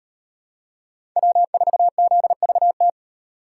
Morse code, heard as a keyed tone, about 36 wpm, 700 Hz. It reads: W4ZVT